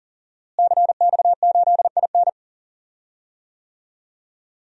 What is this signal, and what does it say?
Morse code, heard as a keyed tone, about 41 wpm, 700 Hz. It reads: CX8IN
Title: DXCC-100 Call Signs - 5 Characters - Repeated 2x - 40wpm